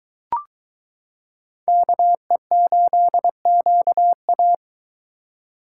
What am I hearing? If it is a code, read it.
Morse code, heard as a keyed tone, about 20 wpm, 700 Hz. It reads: KE8QA